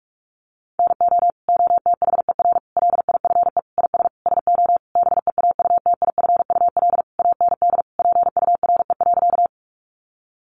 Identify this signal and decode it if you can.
Morse code, heard as a keyed tone, about 44 wpm, 700 Hz. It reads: NO OTHER LIFE IS SO BEAUTIFUL AND PURE.
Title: Sentences from Top 1500 Words 45wpm